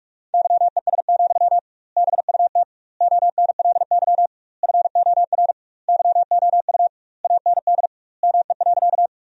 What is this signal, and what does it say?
Morse code, heard as a keyed tone, about 46 wpm, 700 Hz. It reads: YES, BUT ONLY FOR YOU AND ME.
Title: 2x Word-Spacing - Sentences from Top 500 Words - 45wpm